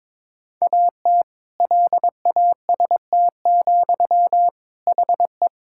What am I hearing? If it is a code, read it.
Morse code, heard as a keyed tone, about 22 wpm, 700 Hz. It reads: AT LAST, HE
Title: Sentences from Top 1000 Words - Review Part 6 - 22wpm